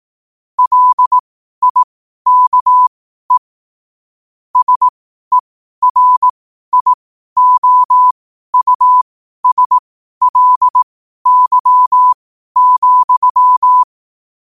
Morse code, heard as a keyed tone, about 18 wpm, spaced as for 12 wpm, 1000 Hz. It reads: LIKE SERIOUSLY,
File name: qotdfiles / 12wpm